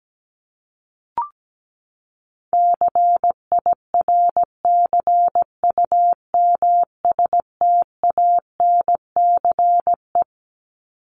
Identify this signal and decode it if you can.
Morse code, heard as a keyed tone, about 17 wpm, 700 Hz. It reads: CIRCUMSTANCE
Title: ES:S Suffix - Top 100 Words with Base Words 17wpm